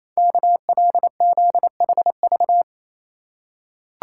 Morse code, heard as a keyed tone, about 27 wpm, 700 Hz. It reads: KLZHV
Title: Code Groups - 5 Letters - 28wpm